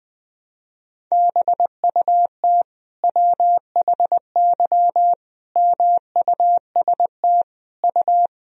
Morse code, heard as a keyed tone, about 21 wpm, 700 Hz. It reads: BUT WHY MUST U
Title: Sentences from Top 400 Words and Abbreviations 20wpm